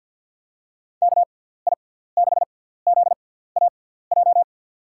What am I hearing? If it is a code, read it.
Morse code, heard as a keyed tone, about 49 wpm, spaced as for 17 wpm, 700 Hz. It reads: KI6ZAJ
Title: USA Call Signs - 6 Characters - 6x Character Spacing - 50wpm